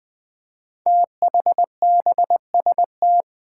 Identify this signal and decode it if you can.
Morse code, heard as a keyed tone, about 20 wpm, 700 Hz. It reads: THBST
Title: Code Groups - 5 Letters - 20wpm